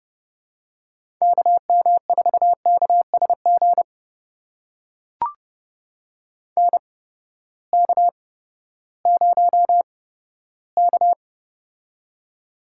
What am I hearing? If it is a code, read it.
Morse code, heard as a keyed tone, about 30 wpm, spaced as for 15 wpm, 700 Hz. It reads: KM4KSG NK0K
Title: USA Call Signs - 8x Character Spacing - 30wpm